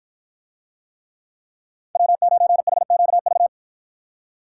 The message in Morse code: W9HQV

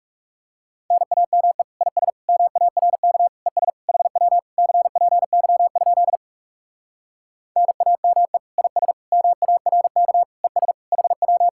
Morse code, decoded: NAME IS MARK ES HW CPY? NAME IS MARK ES HW

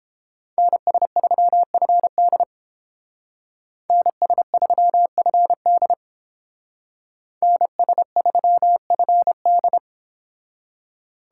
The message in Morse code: NS3FD NS3FD NS3FD

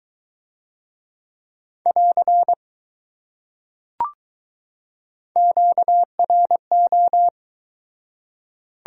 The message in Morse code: <AR> QRO

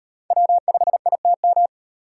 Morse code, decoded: W5ITM

